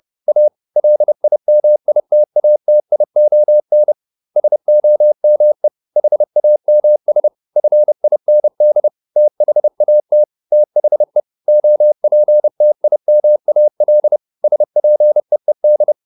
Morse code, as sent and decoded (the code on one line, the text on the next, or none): .- / .-.. .. -- .. - .- - .. --- -. / ... --- -- . / .... .- -- ... / ..-. .. -. -.. / - .... .- - / - .... / --- .--. - .. -- .- .-.. / ... .--. . -..
A LIMITATION SOME HAMS FIND THAT THE OPTIMAL SPEED